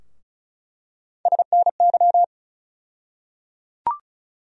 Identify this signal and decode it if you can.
Morse code, heard as a keyed tone, about 34 wpm, 700 Hz. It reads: SNY
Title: USA Call Sign Postfixes - 3 Letters 35wpm